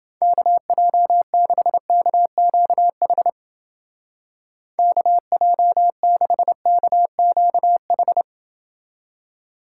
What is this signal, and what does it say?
Morse code, heard as a keyed tone, about 28 wpm, 700 Hz. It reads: KJ6KQH KJ6KQH